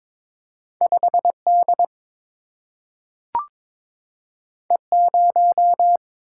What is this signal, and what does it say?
Morse code, heard as a keyed tone, about 22 wpm, 700 Hz. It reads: 5 D E 0
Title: Single-Letter Single-Number 22wpm